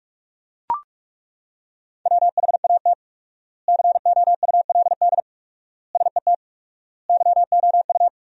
Morse code, heard as a keyed tone, about 42 wpm, 700 Hz. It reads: WHAT COULD SET YOU
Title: 4x Word-Spacing - Sentences from Top 200 Words - 45wpm